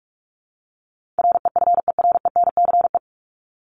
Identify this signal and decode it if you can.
Morse code, heard as a keyed tone, about 44 wpm, 700 Hz. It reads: REFERENCE
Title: Top 1001-1250 Words 45wpm